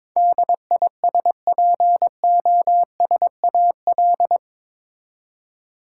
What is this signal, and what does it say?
Morse code, heard as a keyed tone, about 22 wpm, 700 Hz. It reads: DISPOSAL
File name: practice / 22wpm / Eight Letter Words - 22wpm